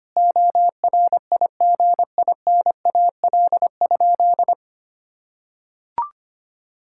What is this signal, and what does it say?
Morse code, heard as a keyed tone, about 25 wpm, 700 Hz. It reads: ORIGINAL?
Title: Sentences from Top 1000 Words - Review Part 6 - 25wpm